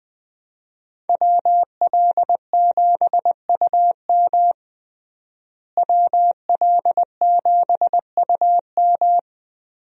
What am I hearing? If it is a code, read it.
Morse code, heard as a keyed tone, about 20 wpm, 700 Hz. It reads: WL7UM WL7UM
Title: USA Call Signs - 5 Characters - Repeated 3x - 20wpm